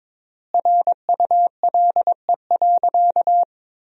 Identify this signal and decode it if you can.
Morse code, heard as a keyed tone, about 22 wpm, 700 Hz. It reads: RULE.
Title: 3x Word-Spacing - Sentences from Top 400 Words - 22wpm